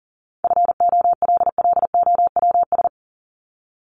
Morse code, heard as a keyed tone, about 39 wpm, 700 Hz. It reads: FOLLOWS
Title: ES:S Suffix - Top 100 Words with Base Words 40wpm